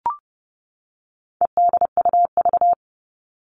Morse code, heard as a keyed tone, about 30 wpm, 700 Hz. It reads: EDUV
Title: All Cut Numbers - 4-Digit 30wpm